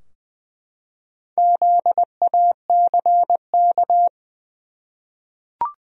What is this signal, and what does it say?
Morse code, heard as a keyed tone, about 20 wpm, 700 Hz. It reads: ZACK
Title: USA Common Names - Max 5 Letters - 20wpm